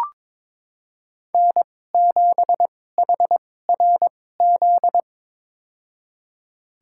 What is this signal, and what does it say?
Morse code, heard as a keyed tone, about 21 wpm, spaced as for 16 wpm, 700 Hz. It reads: N7HRZ
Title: USA Call Signs - 5 Characters - 2x Character Spacing - 22wpm